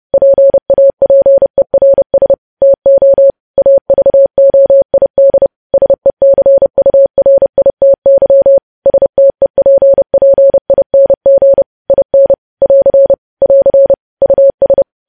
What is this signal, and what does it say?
Morse code, heard as a keyed tone, about 30 wpm, 550 Hz. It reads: PAPERS TO AVOID SECURITY STEPPING IN <AR> <AR> US
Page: news headlines in morse code at 30 wpm